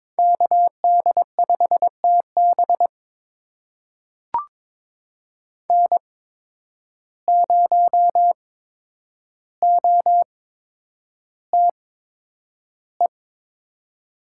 Morse code, heard as a keyed tone, about 22 wpm, spaced as for 9 wpm, 700 Hz. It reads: KD5TB N0OTE
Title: USA Call Signs - 5 Characters - 8x Character Spacing - 22wpm